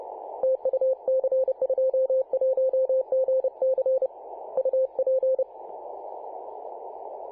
Morse code, heard as a keyed tone, about 30 wpm, 520 Hz. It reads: TU C21GC UP
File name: c21gc-24-895mhz-cw